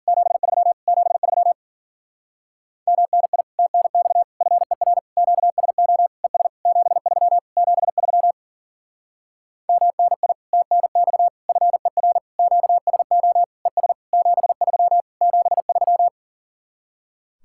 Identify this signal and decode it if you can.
Morse code, heard as a keyed tone, about 54 wpm, 700 Hz. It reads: 73 73 MNI TNX FER QSO ES 73 73 MNI TNX FER QSO ES 73 73